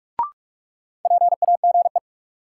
Morse code, heard as a keyed tone, about 46 wpm, 700 Hz. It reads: PAGE
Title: Sentences from Top 1000 Words - Review Part 1 - New Word Drill - 45wpm